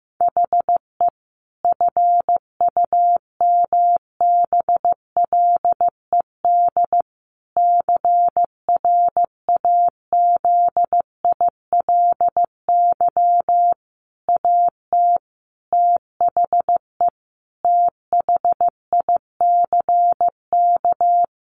Morse code, heard as a keyed tone, about 15 wpm, 700 Hz. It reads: HE FUMBLED CRAZILY AT THE THICK
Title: A Sound Of Thunder 15wpm